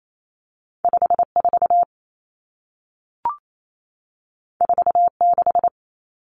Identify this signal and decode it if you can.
Morse code, heard as a keyed tone, about 28 wpm, 700 Hz. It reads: 54 46
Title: Common Cut Numbers - 2-Digit 28wpm